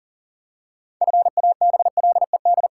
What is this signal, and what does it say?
Morse code, heard as a keyed tone, about 40 wpm, 700 Hz. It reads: FABLED
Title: Pangram Challenge v2 - 40wpm